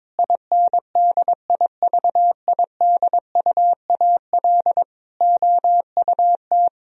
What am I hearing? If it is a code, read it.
Morse code, heard as a keyed tone, about 22 wpm, 700 Hz. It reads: INDIVIDUAL OUT
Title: Sentences from Top 1000 Words - Review Part 4 - 22wpm